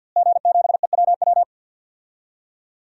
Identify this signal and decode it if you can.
Morse code, heard as a keyed tone, about 52 wpm, 700 Hz. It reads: G7EWW
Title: DXCC-100 Call Signs - 5 Characters - Repeated 2x - 50wpm